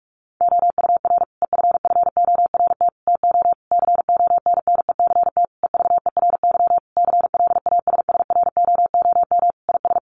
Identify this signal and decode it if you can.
Morse code, heard as a keyed tone, about 44 wpm, 700 Hz. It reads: OUR EFFORT TO CONNECT EVERY CLASSROOM IS
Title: Sentences from Top 1500 Words 45wpm